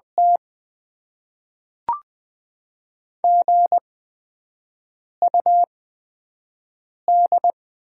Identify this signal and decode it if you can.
Morse code, heard as a keyed tone, about 20 wpm, spaced as for 5 wpm, 700 Hz. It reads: T GUD
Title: Top 20 QSO Elements - 8x Character Spacing - 20wpm